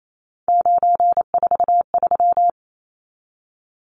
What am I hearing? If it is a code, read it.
Morse code, heard as a keyed tone, about 28 wpm, 700 Hz. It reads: 943
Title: Numbers - 3-Digit 28wpm